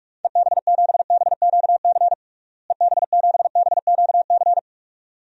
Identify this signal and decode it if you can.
Morse code, heard as a keyed tone, about 46 wpm, 700 Hz. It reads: EB7BQC EB7BQC
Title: DXCC-100 Call Signs - 6 Characters - Repeated 2x - 45wpm